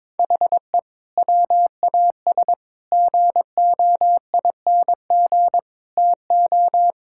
Morse code, heard as a keyed tone, about 22 wpm, 700 Hz. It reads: HE WAS GOING TO
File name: practice / 22wpm / Sentences with 4-Word Phrases - 22wpm